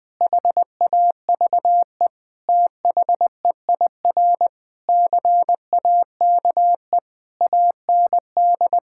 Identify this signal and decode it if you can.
Morse code, heard as a keyed tone, about 20 wpm, 700 Hz. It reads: HAVE THEIR CAKE AND
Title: Sentences with 8-Word Phrases - Spoken Beginning - 20wpm